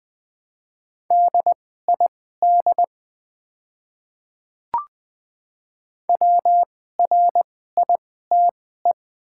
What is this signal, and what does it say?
Morse code, heard as a keyed tone, about 20 wpm, spaced as for 15 wpm, 700 Hz. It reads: DID WRITE
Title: 2x Character Spacing - Top 100 Words - 20wpm